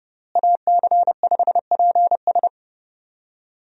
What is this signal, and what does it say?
Morse code, heard as a keyed tone, about 30 wpm, 700 Hz. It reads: AC5PS
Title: USA Call Signs - 5 Characters - Repeated 3x - 30wpm